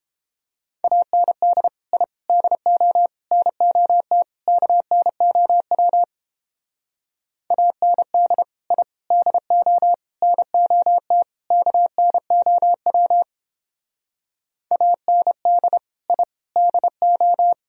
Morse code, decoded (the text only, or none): AND I DO NOT KNOW AND I DO NOT KNOW AND I DO